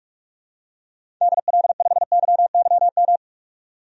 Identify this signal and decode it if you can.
Morse code, heard as a keyed tone, about 42 wpm, 700 Hz. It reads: DL5YYM
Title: DXCC-100 Call Signs - 6 Characters - 45wpm